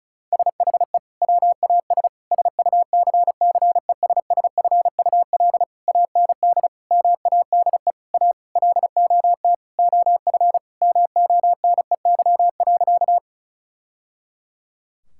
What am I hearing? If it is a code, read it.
Morse code, heard as a keyed tone, about 36 wpm, 700 Hz. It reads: SHE WAS SUCCESSFUL AND MADE A LOT OF MONEY.
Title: Sentences from Top 1000 Words - Review Part 6 - 35wpm